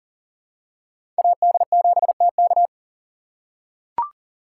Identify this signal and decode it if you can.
Morse code, heard as a keyed tone, about 40 wpm, 700 Hz. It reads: AD7TK
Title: USA Call Signs - 2x Character Spacing - 40wpm